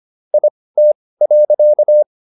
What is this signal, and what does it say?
Morse code, heard as a keyed tone, about 25 wpm, spaced as for 18 wpm, 600 Hz. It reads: IT.